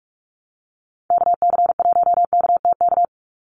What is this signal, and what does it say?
Morse code, heard as a keyed tone, about 44 wpm, 700 Hz. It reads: KC1KTK
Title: USA Call Signs 45wpm